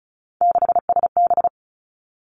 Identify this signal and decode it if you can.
Morse code, heard as a keyed tone, about 36 wpm, 700 Hz. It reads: 6SB
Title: Three Challenging Letter-Number Combinations 35wpm